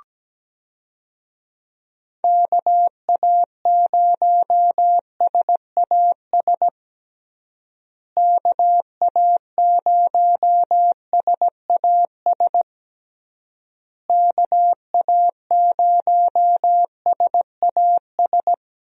Morse code, heard as a keyed tone, about 17 wpm, 700 Hz. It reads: KA0SAS KA0SAS KA0SAS
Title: USA Call Signs - 6 Characters - Repeated 3x - 17wpm